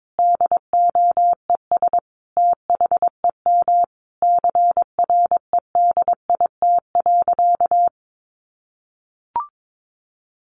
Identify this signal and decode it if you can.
Morse code, heard as a keyed tone, about 22 wpm, 700 Hz. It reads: DOES THEM CREDIT.
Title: Sentences from Top 1000 Words 22wpm